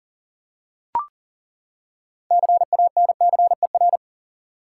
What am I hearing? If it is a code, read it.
Morse code, heard as a keyed tone, about 40 wpm, 700 Hz. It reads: CANCER
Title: Six Letter Words - 40wpm